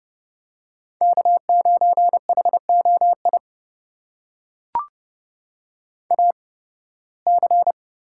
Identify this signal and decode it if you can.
Morse code, heard as a keyed tone, about 30 wpm, spaced as for 20 wpm, 700 Hz. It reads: K9HOI AC
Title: USA Call Signs - 5 Characters - 8x Character Spacing - 30wpm